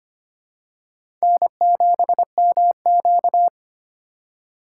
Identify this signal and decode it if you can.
Morse code, heard as a keyed tone, about 25 wpm, 700 Hz. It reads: N7MQ
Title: USA Call Signs - 4 Characters - 4x Character Spacing - 25wpm